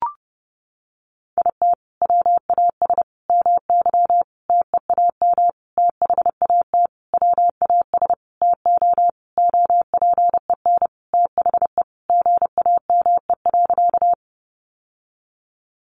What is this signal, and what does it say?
Morse code, heard as a keyed tone, about 30 wpm, 700 Hz. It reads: IT WAS MY TEAM THAT WAS TO OPEN THE GAME.
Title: Sentences from Top 1500 Words 30wpm